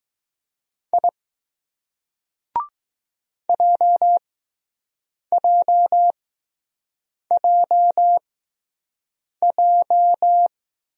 Morse code, heard as a keyed tone, about 20 wpm, spaced as for 6 wpm, 700 Hz. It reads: I JJJJ